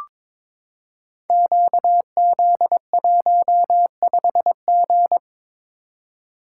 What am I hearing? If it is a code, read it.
Morse code, heard as a keyed tone, about 22 wpm, 690 Hz. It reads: QZ15G
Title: Code Groups - 5 Letter:Numbers - 22wpm